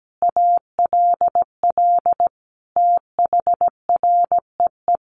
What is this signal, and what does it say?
Morse code, heard as a keyed tone, about 17 wpm, 700 Hz. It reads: ALL THREE
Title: Sentences from Top 1500 Words 17wpm